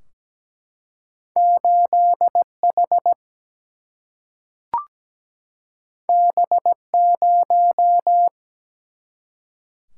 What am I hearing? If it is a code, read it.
Morse code, heard as a keyed tone, about 17 wpm, 700 Hz. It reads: 8H B0
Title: Single-Letter Single-Number 17wpm